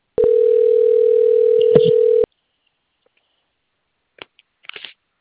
Ringback tone (North America), 440 + 480 Hz.